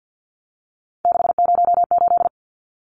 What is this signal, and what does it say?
Morse code, heard as a keyed tone, about 50 wpm, 700 Hz. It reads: 608
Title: Numbers - 3-Digit 50wpm